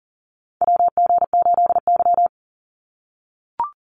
Morse code, heard as a keyed tone, about 40 wpm, 700 Hz. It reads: WG8Y
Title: NA SOTA Chaser Callsigns 40wpm